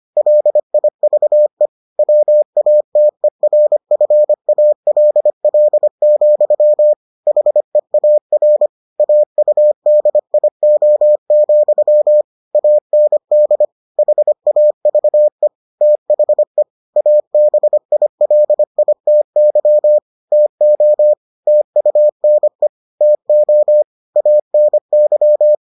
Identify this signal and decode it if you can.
Morse code, heard as a keyed tone, about 25 wpm, 600 Hz. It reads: LIVE WATERFALL, HEAR AUDIO, AND HAVE THE ABILITY TO TUNE TO ANY